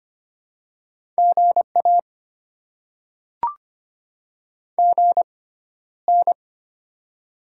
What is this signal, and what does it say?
Morse code, heard as a keyed tone, about 25 wpm, spaced as for 13 wpm, 690 Hz. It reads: GA GN
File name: Top 90 QSO Elements - 6x Character Spacing - 25wpm